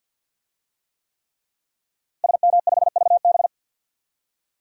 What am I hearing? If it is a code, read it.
Morse code, heard as a keyed tone, about 46 wpm, 700 Hz. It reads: SM5VB
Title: DXCC-100 Call Signs - 5 Characters - Repeated 3x - 50wpm